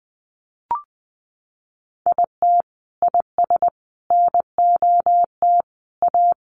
Morse code, heard as a keyed tone, about 19 wpm, 700 Hz. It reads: IT IS NOT A
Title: Sentences from Top 1500 Words 20wpm